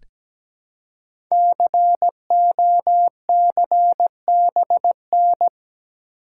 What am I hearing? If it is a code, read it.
Morse code, heard as a keyed tone, about 17 wpm, 700 Hz. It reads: COCBN
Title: Code Groups - 5 Letters - 17wpm